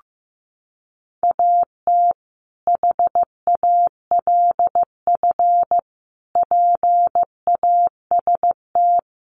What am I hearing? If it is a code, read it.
Morse code, heard as a keyed tone, about 15 wpm, 700 Hz. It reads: AT HALF PAST
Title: Sentences from Top 600 Words 15wpm